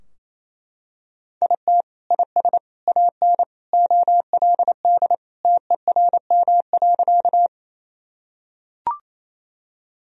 Morse code, heard as a keyed tone, about 28 wpm, 700 Hz. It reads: IT IS AN OLD TERM.
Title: Sentences from Top 1000 Words - Review Part 4 - 28wpm